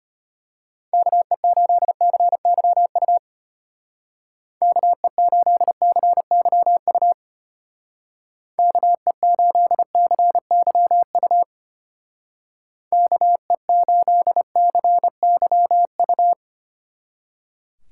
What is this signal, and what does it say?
Morse code, heard as a keyed tone, about 33 wpm, 700 Hz. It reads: KE8CYU KE8CYU KE8CYU KE8CYU